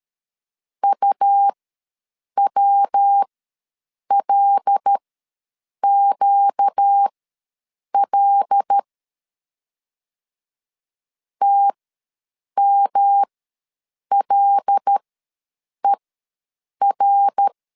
Morse code, heard as a keyed tone, about 13 wpm, spaced as for 7 wpm, 800 Hz. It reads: UWLQL TMLER
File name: LEZIONI / 5 LBVC / 5 L ws 12wpm 2958